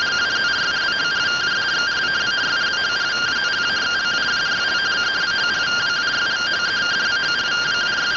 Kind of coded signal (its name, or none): two-tone teleprinter signal